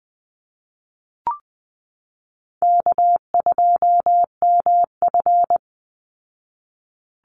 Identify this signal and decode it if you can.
Morse code, heard as a keyed tone, about 20 wpm, 700 Hz. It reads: K2MF
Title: Easy Call Signs Repeated 2x - 20wpm